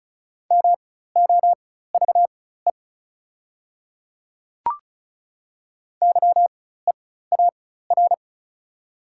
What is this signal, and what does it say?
Morse code, heard as a keyed tone, about 36 wpm, spaced as for 16 wpm, 700 Hz. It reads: MOVE YEAR
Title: 4x Character Spacing - Top 200 Words - 35wpm